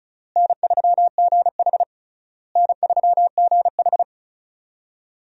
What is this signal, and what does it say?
Morse code, heard as a keyed tone, about 36 wpm, 700 Hz. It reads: N3GH N3GH
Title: USA Call Signs - 4 Characters - Repeated 3x - 35wpm